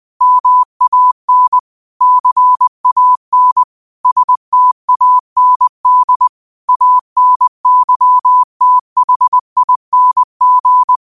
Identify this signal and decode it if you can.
Morse code, heard as a keyed tone, about 20 wpm, 1000 Hz. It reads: MAN CAN STAND ANYTHING
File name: qotdfiles / 20wpm